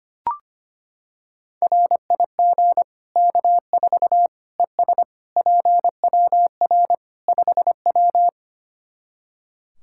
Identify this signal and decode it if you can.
Morse code, heard as a keyed tone, about 25 wpm, 700 Hz. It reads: RIG K4 ES PWR 5W
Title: K1USN SST QSO Practice 25wpm